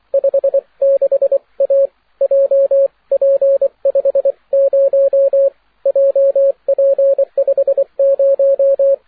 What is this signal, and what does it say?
Morse code, heard as a keyed tone, about 24 wpm, 550 Hz, with radio noise. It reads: 56A JP50 JP50
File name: wp / gi4owa la8aja 6m cw aurora march 11th